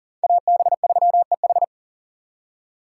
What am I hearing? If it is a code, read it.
Morse code, heard as a keyed tone, about 51 wpm, 700 Hz. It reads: AB3EH